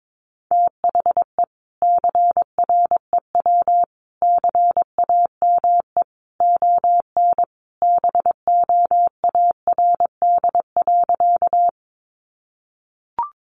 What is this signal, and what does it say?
Morse code, heard as a keyed tone, about 22 wpm, 700 Hz. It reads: THE CREW CAME ON BOARD.
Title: Sentences from Top 1500 Words 22wpm